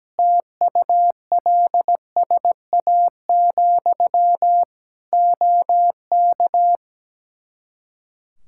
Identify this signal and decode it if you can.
Morse code, heard as a keyed tone, about 17 wpm, 700 Hz. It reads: TULSA, OK
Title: Practice QSO - Exchange 1 - Beginning - 17wpm